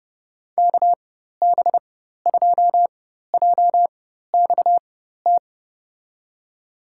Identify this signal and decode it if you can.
Morse code, heard as a keyed tone, about 32 wpm, spaced as for 14 wpm, 700 Hz. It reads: KB2JXT
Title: USA Call Signs - 6 Characters - 4x Character Spacing - 30wpm